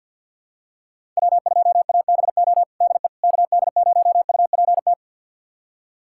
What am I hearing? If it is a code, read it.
Morse code, heard as a keyed tone, about 51 wpm, 700 Hz. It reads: W2ABO DE KD0UPT
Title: USA Callsign DE Callsign 50wpm